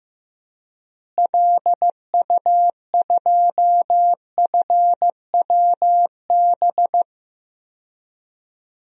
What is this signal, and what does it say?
Morse code, heard as a keyed tone, about 15 wpm, 700 Hz. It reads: LU2FWB
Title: DXCC-100 Call Signs - 6 Characters - Repeated 2x - 15wpm